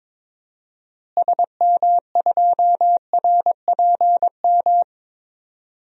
Morse code, heard as a keyed tone, about 21 wpm, 700 Hz. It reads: SM2RPM
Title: DXCC-100 Call Signs - 6 Characters - Repeated 3x - 22wpm